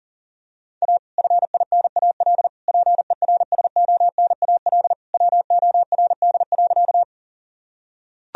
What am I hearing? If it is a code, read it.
Morse code, heard as a keyed tone, about 40 wpm, 700 Hz. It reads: A FINAL PERSONAL WORD.
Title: Sentences from Top 1000 Words - Review Part 4 - 40wpm